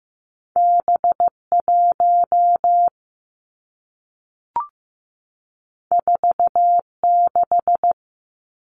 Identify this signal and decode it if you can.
Morse code, heard as a keyed tone, about 15 wpm, 700 Hz. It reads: B 1 4 6
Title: Two Challenging Letter-Number Combinations 15wpm